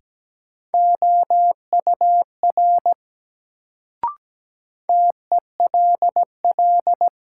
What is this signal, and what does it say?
Morse code, heard as a keyed tone, about 17 wpm, 700 Hz. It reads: OUR TELL
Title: Sets of 2 Words - New Word Drill - 17wpm